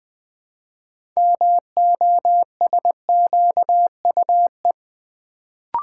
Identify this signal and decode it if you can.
Morse code, heard as a keyed tone, about 20 wpm, 700 Hz. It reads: MOSQUE
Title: Build A Word - 3 Letters At A Time - 6-Letter Words - 20wpm